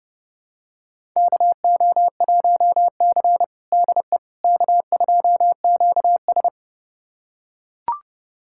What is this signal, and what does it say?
Morse code, heard as a keyed tone, about 30 wpm, 700 Hz. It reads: KO1C DE K2QS
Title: USA Callsign DE Callsign - 4 Characters - 30wpm